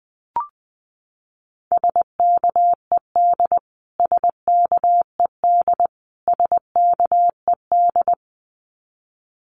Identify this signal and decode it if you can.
Morse code, heard as a keyed tone, about 20 wpm, 700 Hz. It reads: SKED SKED SKED
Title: Abbreviations Repeated 3x - 20wpm